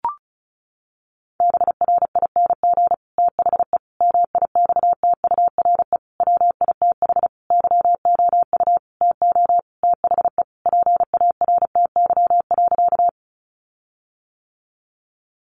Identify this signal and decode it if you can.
Morse code, heard as a keyed tone, about 34 wpm, 700 Hz. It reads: BRING THE MIXTURE WITH YOU TO THE PARTY.
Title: Sentences from Top 1500 Words 35wpm